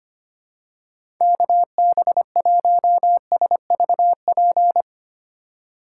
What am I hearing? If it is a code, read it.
Morse code, heard as a keyed tone, about 25 wpm, 700 Hz. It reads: KB1SVP
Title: USA Call Signs - 6 Characters - 3x Character Spacing - 25wpm